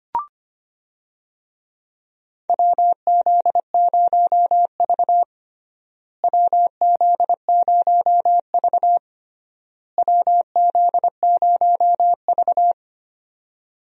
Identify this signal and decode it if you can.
Morse code, heard as a keyed tone, about 25 wpm, 700 Hz. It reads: WZ0V WZ0V WZ0V
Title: USA Call Signs - 4 Characters - Repeated 3x - 25wpm